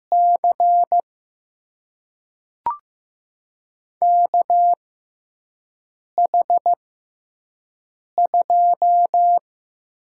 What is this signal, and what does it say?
Morse code, heard as a keyed tone, about 15 wpm, spaced as for 5 wpm, 700 Hz. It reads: C KH2
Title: USA Call Signs - 6x Character Spacing - 15wpm